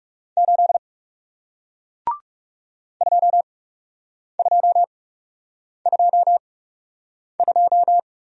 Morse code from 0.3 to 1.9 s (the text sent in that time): 8